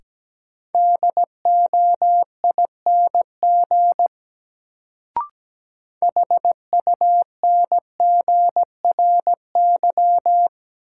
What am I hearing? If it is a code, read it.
Morse code, heard as a keyed tone, about 17 wpm, 700 Hz. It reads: DOING HUNGRY
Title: Sentences from Top 1400 Words - New Word Drill - 17wpm